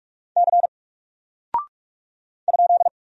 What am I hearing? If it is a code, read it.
Morse code, heard as a keyed tone, about 44 wpm, 700 Hz. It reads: C ?
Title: Single Character (Letter:Number:Punctuation:Prosign) 45wpm